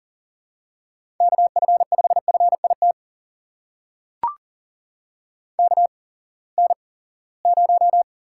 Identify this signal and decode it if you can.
Morse code, heard as a keyed tone, about 42 wpm, spaced as for 23 wpm, 700 Hz. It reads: KF5FIT KN0
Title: USA Call Signs - 6 Characters - 8x Character Spacing - 40wpm